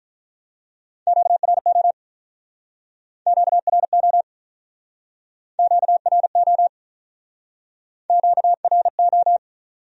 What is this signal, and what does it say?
Morse code, heard as a keyed tone, about 47 wpm, 700 Hz. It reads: QRO QRO QRO QRO